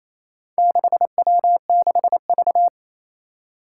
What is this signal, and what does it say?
Morse code, heard as a keyed tone, about 29 wpm, 700 Hz. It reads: 6W6V
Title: Four Challenging Letter-Number Combinations 28wpm